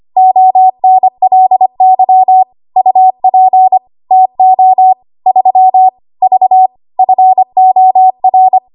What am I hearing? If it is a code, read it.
Morse code, heard as a keyed tone, about 25 wpm, 750 Hz. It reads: ONLY UP TO 3 V FOR
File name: Archive / 25 WPM / 25WPM